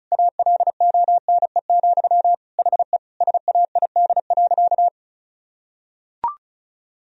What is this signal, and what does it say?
Morse code, heard as a keyed tone, about 36 wpm, 700 Hz. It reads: ALONE, HE SAID.
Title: Sentences from Top 1000 Words - Review Part 2 - 35wpm